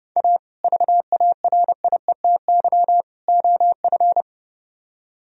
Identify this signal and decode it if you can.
Morse code, heard as a keyed tone, about 35 wpm, 700 Hz. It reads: A VARIETY OF
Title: Speed-Racing - Sets of 3 Words 30wpm